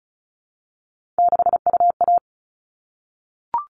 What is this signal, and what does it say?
Morse code, heard as a keyed tone, about 34 wpm, 700 Hz. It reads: 6UA
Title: All Cut Numbers - 3-Digit 35wpm